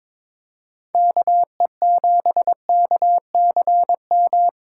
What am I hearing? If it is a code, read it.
Morse code, heard as a keyed tone, about 22 wpm, 700 Hz. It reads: KE7KCM
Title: USA Call Signs - 6 Characters - 2x Character Spacing - 22wpm